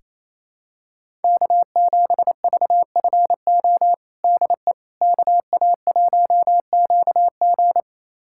Morse code, heard as a keyed tone, about 28 wpm, 700 Hz. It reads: K7VFO DE KA1QG
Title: USA Callsign DE Callsign - 5 Characters - 28wpm